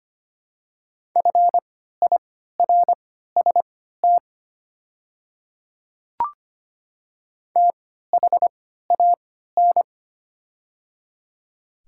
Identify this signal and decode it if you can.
Morse code, heard as a keyed tone, about 25 wpm, spaced as for 14 wpm, 700 Hz. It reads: FIRST THAN
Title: 3x Character Spacing - Top 100 Words - 25wpm